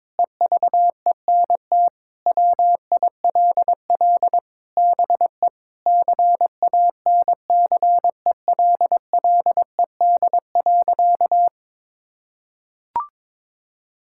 Morse code, heard as a keyed tone, about 22 wpm, 700 Hz. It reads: EVENT WILL BE CANCELLED.
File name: practice / 22wpm / Sentences with 8-Word Phrases - 22wpm